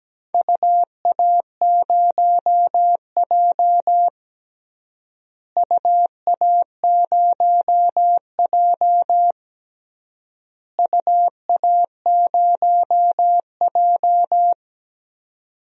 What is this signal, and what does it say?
Morse code, heard as a keyed tone, about 17 wpm, 700 Hz. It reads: UA0J UA0J UA0J
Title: DXCC-100 Call Signs - 4 Characters - Repeated 3x - 17wpm